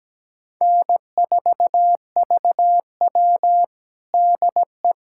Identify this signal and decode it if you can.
Morse code, heard as a keyed tone, about 17 wpm, 700 Hz. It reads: N4VW DE
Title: USA Callsign DE Callsign - 4 Characters - 17wpm